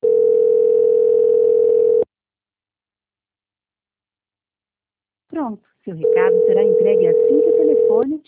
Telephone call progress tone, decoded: ringback tone

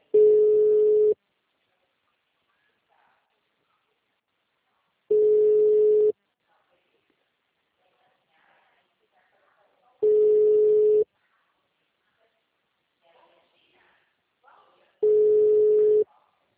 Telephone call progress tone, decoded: ringback tone

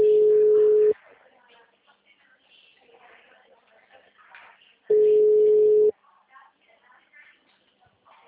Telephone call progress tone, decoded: ringback tone

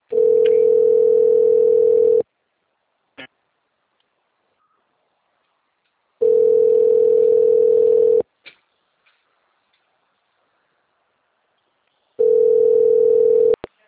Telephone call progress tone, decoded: ringback tone

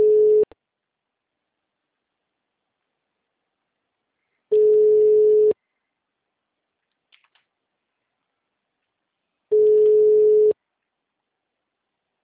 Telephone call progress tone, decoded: ringback tone

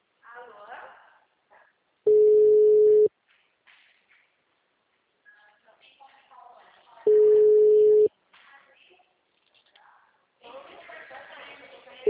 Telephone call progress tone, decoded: ringback tone

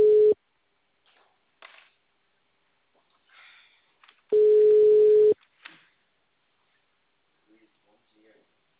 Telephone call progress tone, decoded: ringback tone

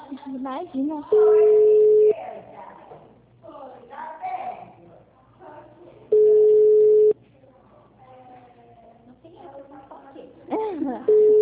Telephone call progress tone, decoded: ringback tone

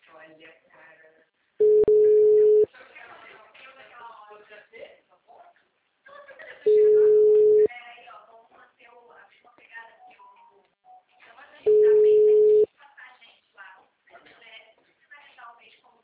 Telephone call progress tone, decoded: ringback tone

